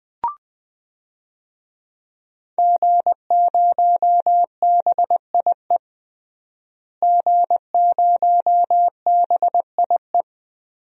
Morse code, heard as a keyed tone, about 21 wpm, 700 Hz. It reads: G0BIE G0BIE
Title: DXCC-100 Call Signs - 5 Characters - Repeated 3x - 20wpm